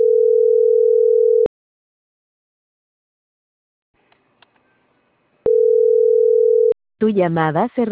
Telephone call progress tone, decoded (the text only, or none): ringback tone